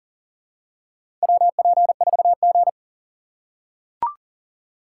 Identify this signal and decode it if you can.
Morse code, heard as a keyed tone, about 40 wpm, 700 Hz. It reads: WP4G
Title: USA Call Signs - 4 Characters - 3x Character Spacing - 40wpm